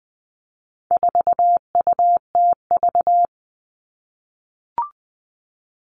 Morse code, heard as a keyed tone, about 20 wpm, 700 Hz. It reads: 4UTV
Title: All Cut Numbers - 4-Digit 20wpm